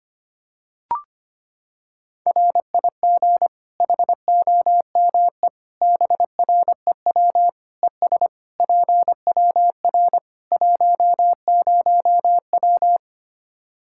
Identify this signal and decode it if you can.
Morse code, heard as a keyed tone, about 25 wpm, 700 Hz. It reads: RIG HOME BREW ES PWR 10W
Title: Practice QSO - Exchange 2 - Info - 25wpm